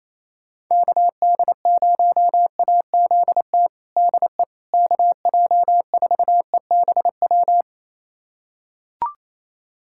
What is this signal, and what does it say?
Morse code, heard as a keyed tone, about 28 wpm, 700 Hz. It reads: KD0AZT DE KJ4EBW
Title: USA Callsign DE Callsign 28wpm